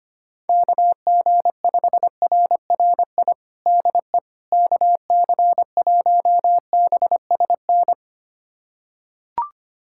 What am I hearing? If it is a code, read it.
Morse code, heard as a keyed tone, about 25 wpm, 700 Hz. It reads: KG5RRI DE KC1BSN
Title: USA Callsign DE Callsign 25wpm